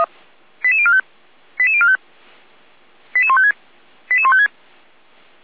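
Five-tone selective call (ZVEI), heard as quick runs of five tones, tones of about 70 ms. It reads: 88045 88045 88156 88156
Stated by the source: ZVEI five-tone call 88249 received by SDR, from feuerwehr-orlamuende.org fire brigade recording server.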